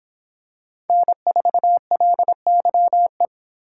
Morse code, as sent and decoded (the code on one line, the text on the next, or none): -. ....- .-.. -.-- .
N4LYE